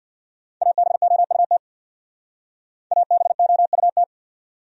Morse code, heard as a keyed tone, about 63 wpm, 700 Hz. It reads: ABOUT ABOUT